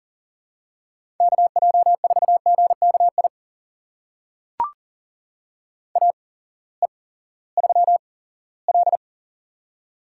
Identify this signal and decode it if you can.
Morse code, heard as a keyed tone, about 40 wpm, spaced as for 20 wpm, 700 Hz. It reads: KJ4GKI AE3L
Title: USA Call Signs - 8x Character Spacing - 40wpm